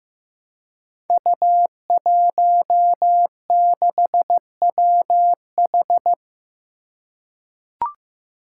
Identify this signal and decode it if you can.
Morse code, heard as a keyed tone, about 15 wpm, 700 Hz. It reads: U16WH
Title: Five Challenging Letter-Number Combinations 15wpm